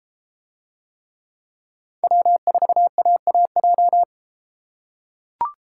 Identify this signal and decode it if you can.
Morse code, heard as a keyed tone, about 30 wpm, 700 Hz. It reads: W4AAJ